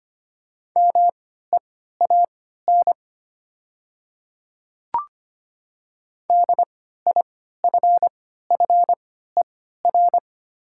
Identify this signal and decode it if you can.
Morse code, heard as a keyed tone, about 25 wpm, spaced as for 14 wpm, 700 Hz. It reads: MEAN DIFFER
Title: 3x Character Spacing - Top 200 Words - 25wpm